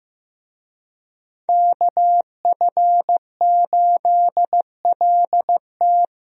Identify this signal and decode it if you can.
Morse code, heard as a keyed tone, about 16 wpm, 700 Hz. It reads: KF8LT
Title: USA Call Signs - 5 Characters - Repeated 3x - 15wpm